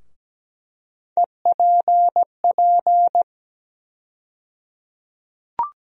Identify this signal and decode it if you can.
Morse code, heard as a keyed tone, about 17 wpm, 700 Hz. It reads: EPP
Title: USA Call Sign Postfixes - 3 Letters 17wpm